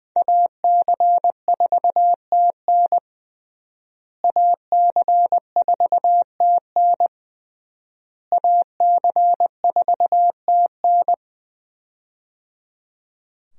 Morse code, heard as a keyed tone, about 20 wpm, 700 Hz. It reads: AC4TN AC4TN AC4TN
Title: USA Call Signs - 5 Characters - Repeated 3x - 20wpm